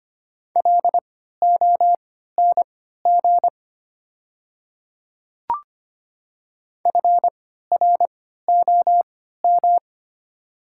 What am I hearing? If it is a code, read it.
Morse code, heard as a keyed tone, about 25 wpm, spaced as for 14 wpm, 700 Hz. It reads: LONG FROM
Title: 3x Character Spacing - Top 100 Words - 25wpm